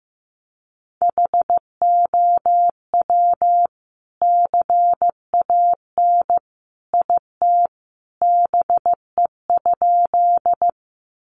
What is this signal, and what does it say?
Morse code, heard as a keyed tone, about 15 wpm, 700 Hz. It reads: HOW CAN IT BE?
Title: Sentences from Top 100 Words 15wpm